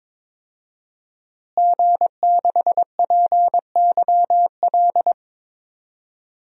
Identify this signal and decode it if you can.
Morse code, heard as a keyed tone, about 22 wpm, 700 Hz. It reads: G6PYL
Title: DXCC-100 Call Signs - 5 Characters - Repeated 2x - 22wpm